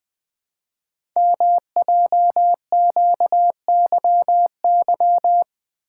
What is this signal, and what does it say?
Morse code, heard as a keyed tone, about 20 wpm, 700 Hz. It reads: MJQYY
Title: Code Groups - 5 Letters - 20wpm